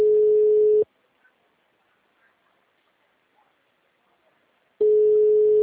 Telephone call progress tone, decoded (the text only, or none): ringback tone